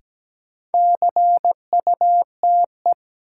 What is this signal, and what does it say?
Morse code, heard as a keyed tone, about 17 wpm, 700 Hz. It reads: CUTE
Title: Sentences from Top 1000 Words - Review Part 7 - New Word Drill - 17wpm